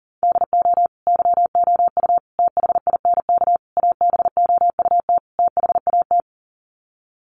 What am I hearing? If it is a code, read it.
Morse code, heard as a keyed tone, about 40 wpm, 700 Hz. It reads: DO YOU THINK ABOUT THAT
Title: Sets of 5 Words Encore2 40wpm